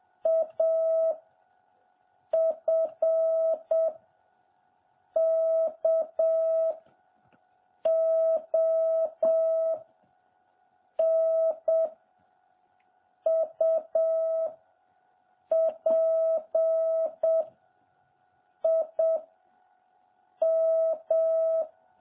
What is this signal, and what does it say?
Morse code, heard as a keyed tone, about 7 wpm, 660 Hz. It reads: A F K O N U P I M
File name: jh8cbh / 2021cw